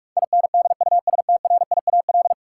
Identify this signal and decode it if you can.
Morse code, heard as a keyed tone, about 46 wpm, 700 Hz. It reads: INDUSTRIAL
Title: Sets of 5 Words - New Word Drill - 45wpm